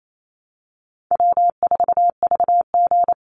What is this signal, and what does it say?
Morse code, heard as a keyed tone, about 28 wpm, 700 Hz. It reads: W4VG
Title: Easy Call Signs Repeated 3x - 28wpm